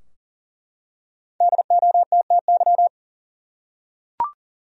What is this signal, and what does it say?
Morse code, heard as a keyed tone, about 40 wpm, 700 Hz. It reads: DOTTY
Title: USA Common Names - 40wpm